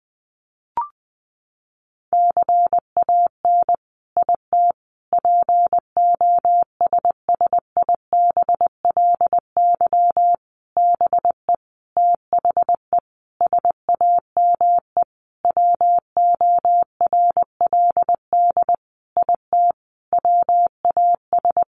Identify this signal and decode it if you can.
Morse code, heard as a keyed tone, about 20 wpm, 700 Hz. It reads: CAN IT POSSIBLY BE THE SAME WORLD IT WAS
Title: Sentences from Top 1300 Words 20wpm